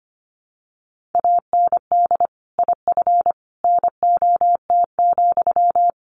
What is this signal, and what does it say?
Morse code, heard as a keyed tone, about 25 wpm, 700 Hz. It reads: AND IF NOT,
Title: Sentences from Top 1400 Words 25wpm